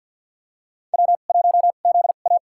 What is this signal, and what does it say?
Morse code, heard as a keyed tone, about 50 wpm, spaced as for 36 wpm, 700 Hz. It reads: W1ZA
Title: USA Call Signs - 4 Characters - 2x Character Spacing - 50wpm